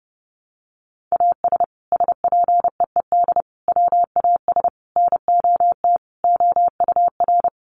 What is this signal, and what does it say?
Morse code, heard as a keyed tone, about 30 wpm, 700 Hz. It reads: AS SPEED WAS NOT OUR
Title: Sentences from Top 800 Words 30wpm